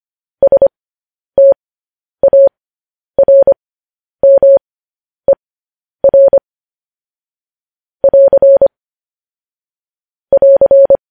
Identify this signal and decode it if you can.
Morse code, heard as a keyed tone, about 25 wpm, spaced as for 10 wpm, 550 Hz. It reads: STARMER <AR> <AR>
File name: podcast / News headlines-1x-25x10